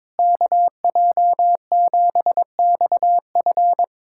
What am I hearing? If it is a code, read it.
Morse code, heard as a keyed tone, about 22 wpm, 700 Hz. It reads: KJ7XF
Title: USA Callsign DE Callsign - 5 Characters - 22wpm